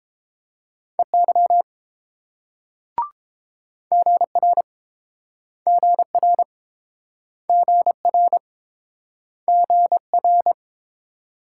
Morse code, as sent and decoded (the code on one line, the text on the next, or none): . -.-- / --. .-. / --. .-. / --. .-. / --. .-.
EY GR GR GR GR